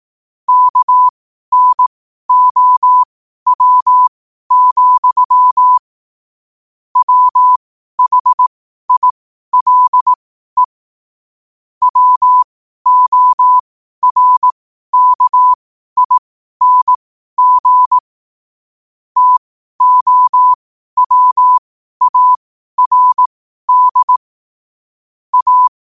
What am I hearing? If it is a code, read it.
Morse code, heard as a keyed tone, about 18 wpm, spaced as for 12 wpm, 1000 Hz. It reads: KNOW, WHILE WORKING TOWARD A
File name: qotdfiles / 12wpm